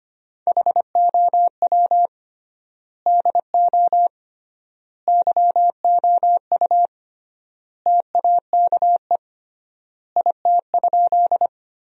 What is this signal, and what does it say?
Morse code, heard as a keyed tone, about 25 wpm, 700 Hz. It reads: HOW DO YOU TAKE IT?
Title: 3x Word-Spacing - Sentences from Top 200 Words - 25wpm